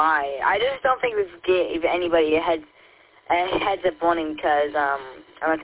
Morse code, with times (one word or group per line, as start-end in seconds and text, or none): none